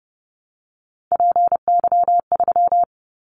Morse code, heard as a keyed tone, about 30 wpm, 700 Hz. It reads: PY3
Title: Callsign Prefixes for Top 100 DXCC - 30wpm